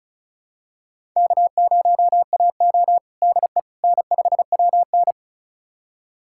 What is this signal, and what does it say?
Morse code, heard as a keyed tone, about 34 wpm, 700 Hz. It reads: K0AO DE N5WN
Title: USA Callsign DE Callsign - 4 Characters - 35wpm